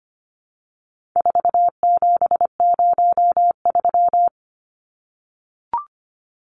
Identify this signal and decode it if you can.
Morse code, heard as a keyed tone, about 25 wpm, 700 Hz. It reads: 4703
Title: Numbers - 4-Digit 25wpm